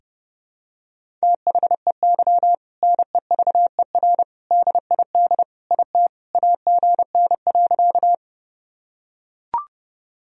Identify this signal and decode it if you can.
Morse code, heard as a keyed tone, about 31 wpm, 700 Hz. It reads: THEY NEVER DID IT AGN.
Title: Sentences from Top 300 Words and Abbreviations 30wpm